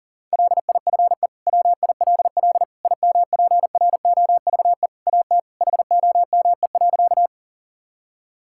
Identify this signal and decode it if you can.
Morse code, heard as a keyed tone, about 40 wpm, 690 Hz. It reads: LIFE WILL IMPROVE AT HOME.
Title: Sentences from Top 1000 Words - Review Part 5 - 40wpm